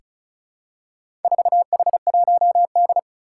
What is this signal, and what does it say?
Morse code, heard as a keyed tone, about 36 wpm, 700 Hz. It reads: 4H1D
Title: Four Challenging Letter-Number Combinations 35wpm